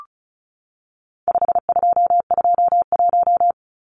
Morse code, heard as a keyed tone, about 34 wpm, 700 Hz. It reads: 5221
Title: Numbers - 4-Digit 35wpm